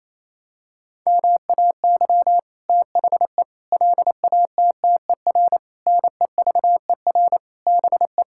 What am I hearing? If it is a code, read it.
Morse code, heard as a keyed tone, about 27 wpm, 700 Hz. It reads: MAY THE LATTER NEVER BE
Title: Sentences from Top 1600 Words 28wpm